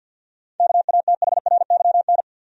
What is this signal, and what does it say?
Morse code, heard as a keyed tone, about 52 wpm, 700 Hz. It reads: KATHRYN
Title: USA Common Names - 50wpm